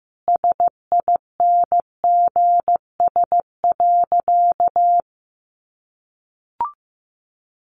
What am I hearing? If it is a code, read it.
Morse code, heard as a keyed tone, about 14 wpm, 700 Hz. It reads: SINGS.
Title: Easy Phrases 15wpm